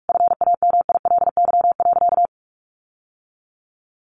Morse code, heard as a keyed tone, about 44 wpm, 700 Hz. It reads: FAMILY.
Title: Sentences from Top 1400 Words 45wpm